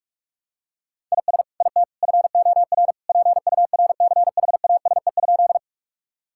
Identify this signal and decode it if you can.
Morse code, heard as a keyed tone, about 46 wpm, 700 Hz. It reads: IS IT FOR PURCHASE?
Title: Sentences from Top 1000 Words - Review Part 5 - 45wpm